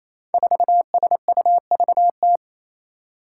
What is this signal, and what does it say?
Morse code, heard as a keyed tone, about 28 wpm, 700 Hz. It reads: 4SUVT
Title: Five Challenging Letter-Number Combinations 28wpm